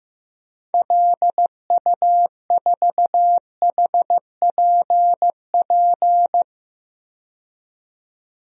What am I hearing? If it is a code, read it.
Morse code, heard as a keyed tone, about 15 wpm, 700 Hz. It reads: LU4HPP
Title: DXCC-100 Call Signs - 6 Characters - 15wpm